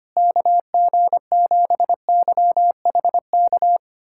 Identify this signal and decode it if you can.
Morse code, heard as a keyed tone, about 25 wpm, 700 Hz. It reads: KG7YHK
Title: USA Callsign DE Callsign 25wpm